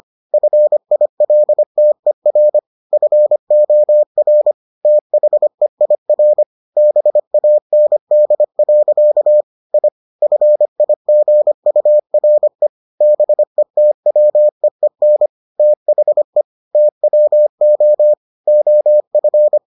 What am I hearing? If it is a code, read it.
Morse code, heard as a keyed tone, about 25 wpm, 600 Hz. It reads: FILTER FOR THEIR BAND. I FIGURE BETWEEN THE TWO OF